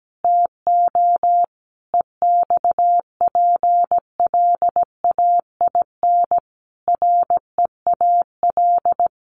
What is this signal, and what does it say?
Morse code, heard as a keyed tone, about 17 wpm, 700 Hz. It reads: TO EXPLAIN REAL
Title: Sentences from Top 1100 Words 17wpm